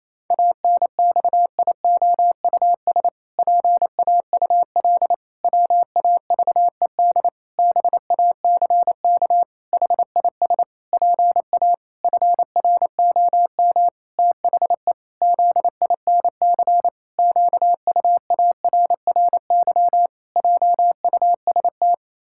Morse code, heard as a keyed tone, about 27 wpm, 700 Hz. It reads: ANXIOUS PAUL WAVED BACK HIS PA FROM THE ZINC QUARRY JUST
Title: Pangram Challenge v2 - 28wpm